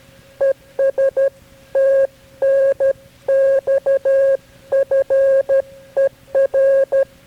Morse code, heard as a keyed tone, about 13 wpm, 540 Hz. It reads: ES TNX FER
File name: KB6MT-tape3-side1